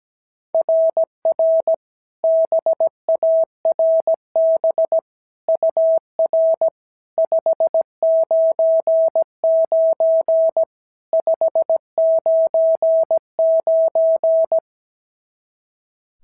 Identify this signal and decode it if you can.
Morse code, heard as a keyed tone, about 16 wpm, 650 Hz. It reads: RR BARB UR 599 599
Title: SOTA QSO Practice 17wpm